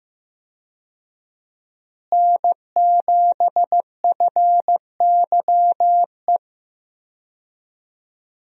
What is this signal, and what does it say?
Morse code, heard as a keyed tone, about 15 wpm, 700 Hz. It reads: N7FYE
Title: USA Call Signs - 5 Characters - Repeated 2x - 15wpm